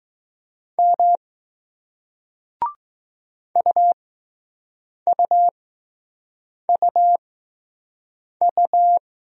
Morse code, heard as a keyed tone, about 20 wpm, 700 Hz. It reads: M U U U U